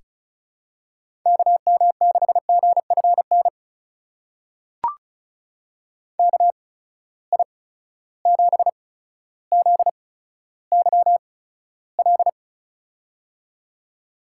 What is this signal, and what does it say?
Morse code, heard as a keyed tone, about 34 wpm, spaced as for 15 wpm, 700 Hz. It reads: KM6GFN KI7ZYL
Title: USA Call Signs - 6 Characters - 8x Character Spacing - 35wpm